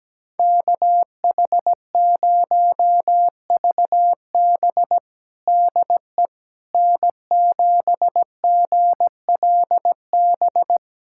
Morse code, heard as a keyed tone, about 17 wpm, 700 Hz. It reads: KH0VB DE N7GLB
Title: USA Callsign DE Callsign - 5 Characters - 17wpm